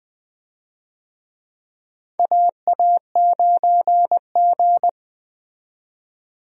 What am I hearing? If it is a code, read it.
Morse code, heard as a keyed tone, about 20 wpm, 700 Hz. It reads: AA9G
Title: USA Call Signs - 4 Characters - 20wpm